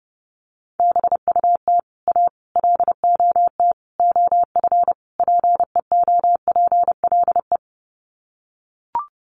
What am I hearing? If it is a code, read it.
Morse code, heard as a keyed tone, about 30 wpm, 700 Hz. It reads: BUT A LOT OF PEOPLE
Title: Sets of 5 Words Encore2 30wpm